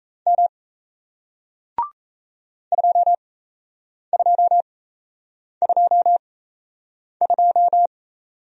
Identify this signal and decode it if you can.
Morse code, heard as a keyed tone, about 37 wpm, spaced as for 8 wpm, 700 Hz. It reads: M 2222